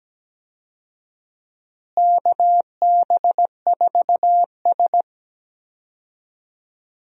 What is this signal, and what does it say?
Morse code, heard as a keyed tone, about 17 wpm, 700 Hz. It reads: KB4S
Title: USA Call Signs - 4 Characters - Repeated 2x - 17wpm